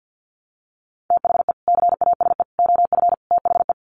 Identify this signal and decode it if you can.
Morse code, heard as a keyed tone, about 50 wpm, 700 Hz. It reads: THE CASE OF THE
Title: Sets of 4 Words 50wpm